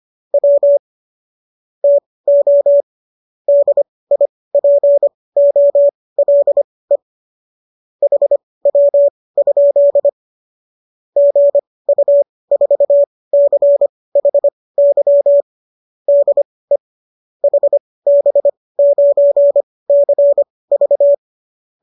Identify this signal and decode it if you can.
Morse code, heard as a keyed tone, about 25 wpm, spaced as for 18 wpm, 570 Hz. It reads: W TO DIPOLE HW? GU4CHY DE HB9CV